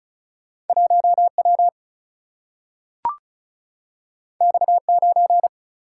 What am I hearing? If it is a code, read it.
Morse code, heard as a keyed tone, about 36 wpm, 700 Hz. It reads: 1W X9
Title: Single-Letter Single-Number 35wpm